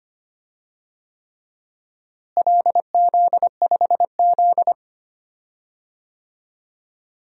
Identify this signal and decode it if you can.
Morse code, heard as a keyed tone, about 25 wpm, 700 Hz. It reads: LZ5Z
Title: DXCC-100 Call Signs - 4 Characters - Repeated 3x - 25wpm